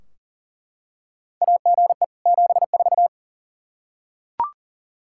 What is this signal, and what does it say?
Morse code, heard as a keyed tone, about 40 wpm, 700 Hz. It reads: AGE 74
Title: Practice QSO - Exchange 3 - More Info - 40wpm